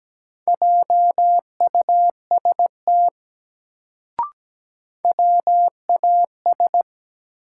Morse code, heard as a keyed tone, about 17 wpm, 700 Hz. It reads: JUST WAS
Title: Sets of 2 Words - New Word Drill - 17wpm